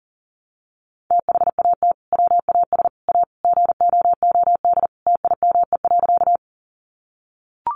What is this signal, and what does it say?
Morse code, heard as a keyed tone, about 40 wpm, 700 Hz. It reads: THAT WAS A GOOD TIME.
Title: Sentences from Top 200 Words 40wpm